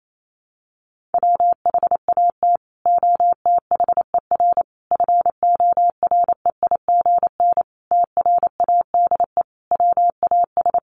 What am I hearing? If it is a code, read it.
Morse code, heard as a keyed tone, about 28 wpm, 700 Hz. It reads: WHAT OTHER FOREIGN TRADE WAS
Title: Sentences from Top 1400 Words 28wpm